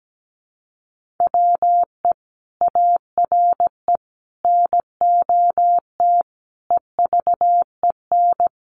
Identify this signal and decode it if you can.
Morse code, heard as a keyed tone, about 17 wpm, 700 Hz. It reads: WE ARE NOT EVEN
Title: Sentences from Top 1500 Words 17wpm